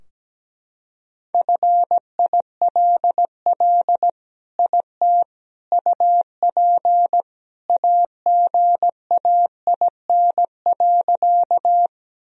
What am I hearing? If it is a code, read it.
Morse code, heard as a keyed tone, about 17 wpm, 700 Hz. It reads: FILL IT UP AGAIN.
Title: Sentences from Top 1000 Words - Review Part 3 - 17wpm